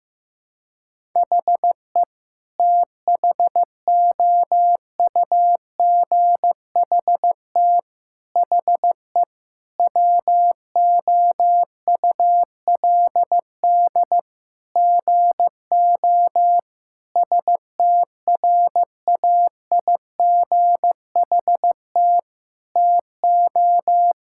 Morse code, heard as a keyed tone, about 15 wpm, 700 Hz. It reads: HE THOUGHT HE WOULD GO STRAIGHT TO
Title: Sentences from Top 1000 Words - Review Part 6 - 15wpm